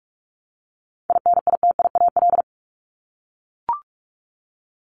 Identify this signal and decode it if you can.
Morse code, heard as a keyed tone, about 44 wpm, 700 Hz. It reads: INITIAL
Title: Top 1501-1750 Words 45wpm